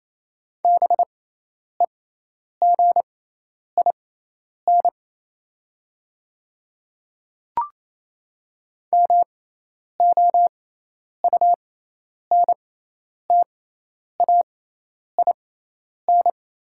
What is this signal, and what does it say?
Morse code, heard as a keyed tone, about 28 wpm, spaced as for 10 wpm, 690 Hz. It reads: BEGIN MOUNTAIN
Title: 6x Character Spacing - Top 300 Words - 28wpm